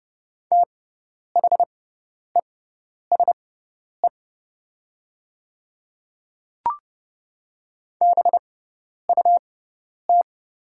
Morse code, heard as a keyed tone, about 30 wpm, spaced as for 10 wpm, 700 Hz. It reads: THESE BUT
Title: 6x Character Spacing - Top 300 Words - 30wpm